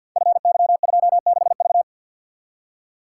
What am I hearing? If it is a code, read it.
Morse code, heard as a keyed tone, about 52 wpm, 700 Hz. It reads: FY26V
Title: Code Groups - 5 Letter:Numbers - 50wpm